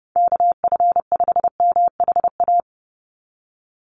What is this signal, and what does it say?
Morse code, heard as a keyed tone, about 30 wpm, 700 Hz. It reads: KF5MHA
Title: USA Call Signs 30wpm